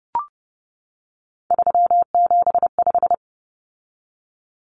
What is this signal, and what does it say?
Morse code, heard as a keyed tone, about 30 wpm, 700 Hz. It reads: 375
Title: Common Cut Numbers - 3-Digit 30wpm